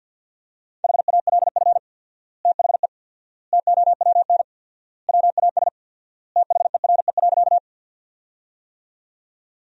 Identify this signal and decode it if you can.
Morse code, heard as a keyed tone, about 53 wpm, 700 Hz. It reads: HALF THE TOWN WAS THERE.
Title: 4x Word-Spacing - Sentences from Top 500 Words - 50wpm